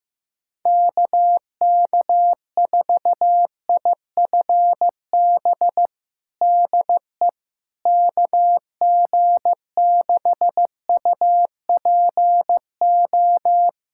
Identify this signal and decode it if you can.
Morse code, heard as a keyed tone, about 15 wpm, 700 Hz. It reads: KK4IFB DE KG6UPO
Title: USA Callsign DE Callsign - 6 Characters - 15wpm